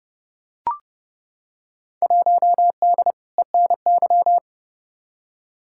Morse code, heard as a keyed tone, about 30 wpm, 700 Hz. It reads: 1D ENY
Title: Field Day Exchanges - 30wpm